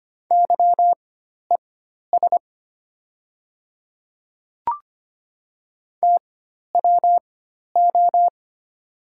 Morse code, heard as a keyed tone, about 25 wpm, spaced as for 12 wpm, 690 Hz. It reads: YES TWO
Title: 4x Character Spacing - Top 500 Words - 25wpm